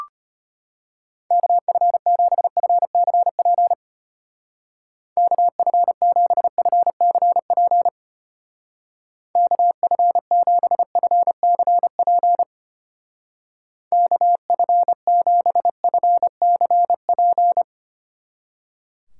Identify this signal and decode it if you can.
Morse code, heard as a keyed tone, about 31 wpm, 700 Hz. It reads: KF7FCP KF7FCP KF7FCP KF7FCP
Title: Speed Racing - USA Call Signs 25wpm